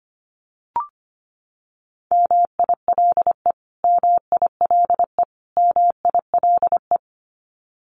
Morse code, heard as a keyed tone, about 25 wpm, 700 Hz. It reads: MILE MILE MILE
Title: Top 300 Words Repeated 25wpm